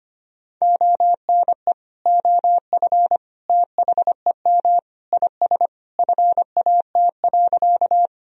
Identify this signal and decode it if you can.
Morse code, heard as a keyed tone, about 25 wpm, 700 Hz. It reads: ONE OF THEM IS FAT.
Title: Sentences from Top 1000 Words - Review Part 4 - 25wpm